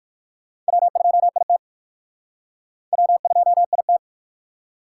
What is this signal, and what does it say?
Morse code, heard as a keyed tone, about 59 wpm, 700 Hz. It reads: W2IT W2IT